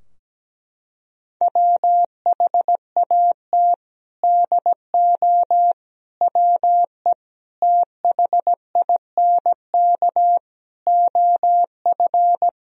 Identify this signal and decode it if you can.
Morse code, heard as a keyed tone, about 17 wpm, 700 Hz. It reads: WHAT DO WE THINK OF
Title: Sentences from Top 1000 Words - Review Part 2 - 17wpm